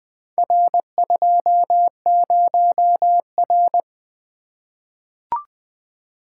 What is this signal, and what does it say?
Morse code, heard as a keyed tone, about 20 wpm, 700 Hz. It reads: R20R
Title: Four Letter-Number Combinations - 20wpm